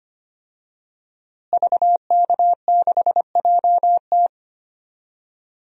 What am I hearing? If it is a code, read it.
Morse code, heard as a keyed tone, about 25 wpm, 700 Hz. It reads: VK6JT
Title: DXCC-100 Call Signs - 5 Characters - Repeated 2x - 25wpm